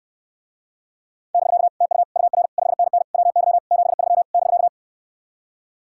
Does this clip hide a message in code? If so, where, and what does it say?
Yes: Morse code from 1.3 to 4.7 s, heard as a keyed tone, about 69 wpm, 700 Hz. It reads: <BK> TU UR 5NN WY 73 <BK>